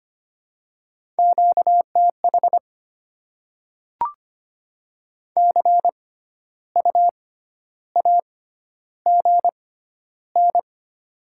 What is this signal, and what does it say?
Morse code, heard as a keyed tone, about 25 wpm, spaced as for 12 wpm, 700 Hz. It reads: QTH CU AGN
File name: Top 20 QSO Elements - 6x Character Spacing - 25wpm